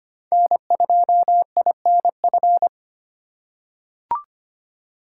Morse code, heard as a keyed tone, about 25 wpm, 700 Hz. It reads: N2INF
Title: USA Call Signs - 2x Character Spacing - 25wpm